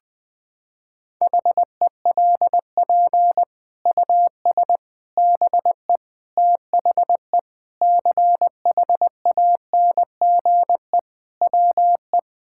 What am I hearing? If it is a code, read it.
Morse code, heard as a keyed tone, about 20 wpm, 700 Hz. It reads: HELP US BE THE CHANGE WE
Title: Sets of 9 Words - 20wpm